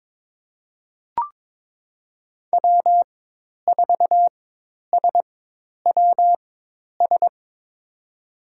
Morse code, heard as a keyed tone, about 21 wpm, spaced as for 10 wpm, 700 Hz. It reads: W4SWS
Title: USA Call Signs - 5 Characters - 4x Character Spacing - 22wpm